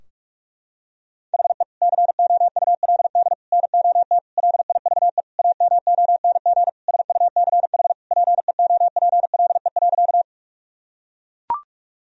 Morse code, heard as a keyed tone, about 41 wpm, 700 Hz. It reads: HE COULD NOT LIVE AMONG SUCH PEOPLE.
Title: Sentences from Top 1000 Words - Review Part 3 - 45wpm